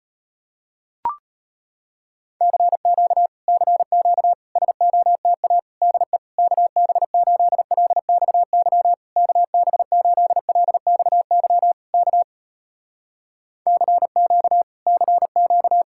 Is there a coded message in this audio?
Yes: Morse code from 2.4 to 15.8 s, heard as a keyed tone, about 36 wpm, 700 Hz. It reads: CQ CQ SOTA DE KB8LXY KB8LXY K CQ CQ